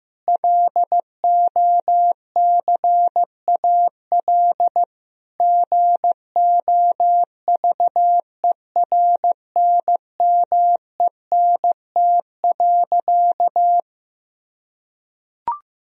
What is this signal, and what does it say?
Morse code, heard as a keyed tone, about 16 wpm, 700 Hz. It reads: LOCAL GOVERNMENT.
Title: Sentences from Top 1000 Words - Review Part 3 - 15wpm